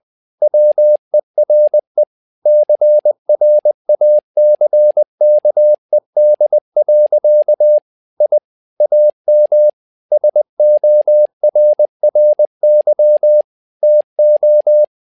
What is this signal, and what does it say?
Morse code, heard as a keyed tone, about 20 wpm, 600 Hz. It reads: WERE CRACKED. I AM SORRY TO
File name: tower trouble 20wpm